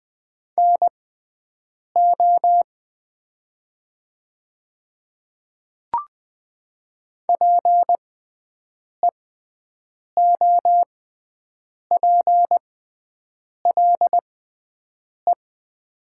Morse code, heard as a keyed tone, about 20 wpm, spaced as for 7 wpm, 700 Hz. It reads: NO PEOPLE
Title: 6x Character Spacing - Top 200 Words - 20wpm